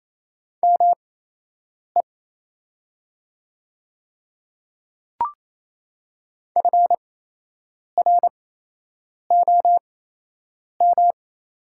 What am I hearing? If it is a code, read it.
Morse code, heard as a keyed tone, about 28 wpm, spaced as for 8 wpm, 700 Hz. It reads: ME FROM
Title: 8x Character Spacing - Top 200 Words - 28wpm